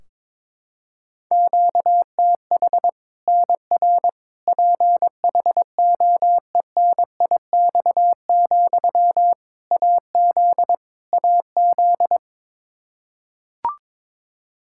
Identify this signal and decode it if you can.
Morse code, heard as a keyed tone, about 22 wpm, 700 Hz. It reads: QTH NR PHOENIX, AZ AZ
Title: Practice QSO - Exchange 1 - Beginning - 22wpm